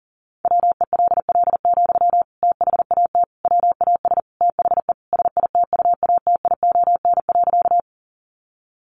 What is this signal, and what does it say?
Morse code, heard as a keyed tone, about 40 wpm, 700 Hz. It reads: WELL, THAT WAS THE SITUATION.
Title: Sentences from Top 1200 Words 40wpm